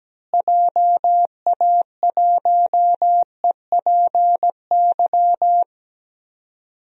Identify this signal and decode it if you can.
Morse code, heard as a keyed tone, about 17 wpm, 700 Hz. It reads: JA1EPY
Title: DXCC-100 Call Signs - 6 Characters - Repeated 2x - 17wpm